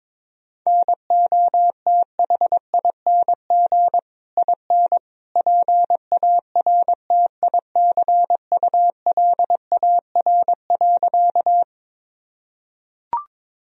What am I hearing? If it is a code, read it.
Morse code, heard as a keyed tone, about 22 wpm, 700 Hz. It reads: NOTHING IN PARTICULAR.
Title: Sentences from Top 1000 Words - Review Part 4 - 22wpm